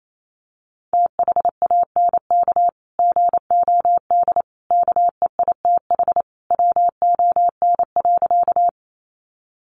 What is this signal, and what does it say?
Morse code, heard as a keyed tone, about 28 wpm, 700 Hz. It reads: THANK GOD KEITH WON.
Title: A Sound Of Thunder 28wpm